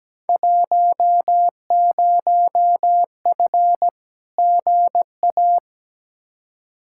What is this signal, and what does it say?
Morse code, heard as a keyed tone, about 17 wpm, 700 Hz. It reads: 10F GA
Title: Field Day Exchanges Encore - 17wpm